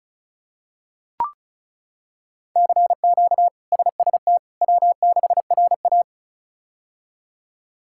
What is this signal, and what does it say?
Morse code, heard as a keyed tone, about 34 wpm, 700 Hz. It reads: CQ SST W6RA
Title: K1USN SST QSO Practice - 35wpm